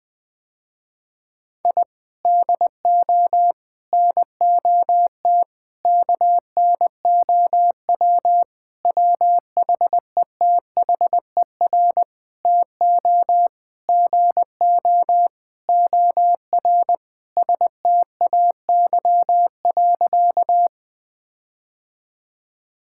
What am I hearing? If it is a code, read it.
Morse code, heard as a keyed tone, about 20 wpm, 700 Hz. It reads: I DO NOT KNOW WHETHER TO GO OR STAY.
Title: Sentences with 5-Word Phrases - Spoken Beginning - 20wpm